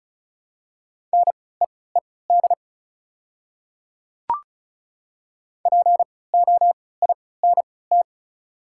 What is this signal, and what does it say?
Morse code, heard as a keyed tone, about 36 wpm, spaced as for 20 wpm, 700 Hz. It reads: NEED POINT
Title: 3x Character Spacing - Top 200 Words - 35wpm